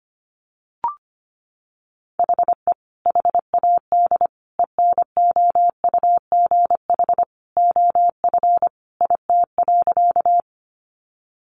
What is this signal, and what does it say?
Morse code, heard as a keyed tone, about 25 wpm, 700 Hz. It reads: HE HAD ENOUGH OF IT.
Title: Sentences from Top 400 Words 25wpm